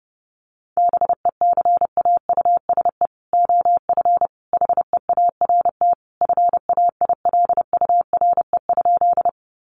Morse code, heard as a keyed tone, about 28 wpm, 700 Hz. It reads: BECAUSE OF HEART FAILURE?
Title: Sentences from Top 1200 Words 30wpm